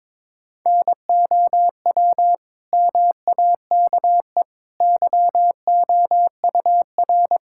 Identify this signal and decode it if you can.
Morse code, heard as a keyed tone, about 22 wpm, 700 Hz. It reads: NOW MAKE YOUR
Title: Sentences from Top 1000 Words - Review Part 6 - 22wpm